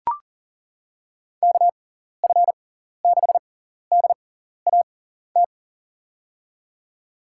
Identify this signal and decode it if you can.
Morse code, heard as a keyed tone, about 37 wpm, spaced as for 14 wpm, 700 Hz. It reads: KF6DAT
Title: USA Call Signs - 6 Characters - 6x Character Spacing - 40wpm